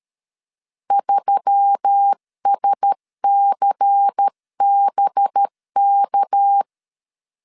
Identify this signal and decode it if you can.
Morse code, heard as a keyed tone, about 13 wpm, 800 Hz. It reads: 3SCBK
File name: CW / DET / 1 6 1500C S C 12wpm 3533